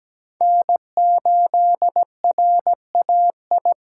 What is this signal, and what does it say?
Morse code, heard as a keyed tone, about 17 wpm, 700 Hz. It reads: N8RAI
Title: USA Call Signs - 5 Characters - 4x Character Spacing - 17wpm